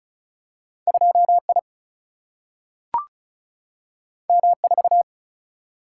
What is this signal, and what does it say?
Morse code, heard as a keyed tone, about 36 wpm, 700 Hz. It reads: 2I M4
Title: Single-Letter Single-Number 35wpm